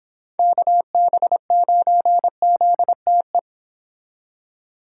Morse code, heard as a keyed tone, about 26 wpm, 700 Hz. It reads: KB9ZTE